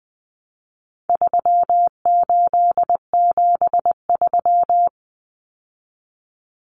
Morse code, heard as a keyed tone, about 20 wpm, 700 Hz. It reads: 3873
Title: Numbers - 4-Digit 20wpm